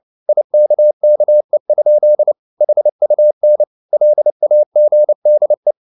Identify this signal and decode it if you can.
Morse code, heard as a keyed tone, about 29 wpm, 600 Hz. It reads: IKKE? HUN LAGDE
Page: CW med Gnister